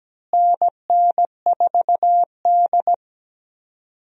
Morse code, heard as a keyed tone, about 17 wpm, 700 Hz. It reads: NN4D
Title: USA Call Signs - 4 Characters - Repeated 3x - 17wpm